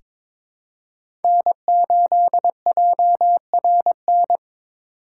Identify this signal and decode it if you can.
Morse code, heard as a keyed tone, about 22 wpm, 700 Hz. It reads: N8JRN
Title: USA Call Signs - 5 Characters - 4x Character Spacing - 22wpm